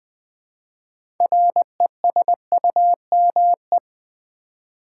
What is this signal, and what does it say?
Morse code, heard as a keyed tone, about 20 wpm, 700 Hz. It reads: RESUME
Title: 6 Letter Words - First 3 Letters Given - 20wpm